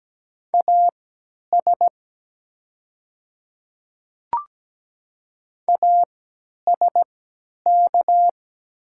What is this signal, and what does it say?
Morse code, heard as a keyed tone, about 17 wpm, spaced as for 10 wpm, 700 Hz. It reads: AS ASK
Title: 3x Character Spacing - Top 300 Words - 17wpm